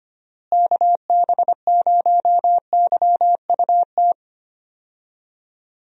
Morse code, heard as a keyed tone, about 25 wpm, 700 Hz. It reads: KB0YUT
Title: USA Call Signs - 6 Characters - Repeated 2x - 25wpm